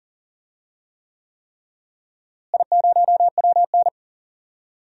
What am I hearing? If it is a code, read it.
Morse code, heard as a keyed tone, about 40 wpm, 700 Hz. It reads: I0WN
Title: DXCC-100 Call Signs - 4 Characters - Repeated 3x - 40wpm